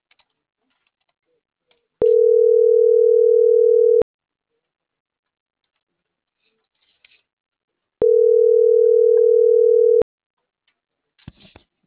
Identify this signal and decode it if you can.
Ringback tone (North America), 440 + 480 Hz.